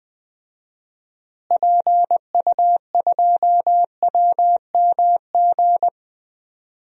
Morse code, heard as a keyed tone, about 20 wpm, 700 Hz. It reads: PU2WMG
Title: DXCC-100 Call Signs - 6 Characters - Repeated 2x - 20wpm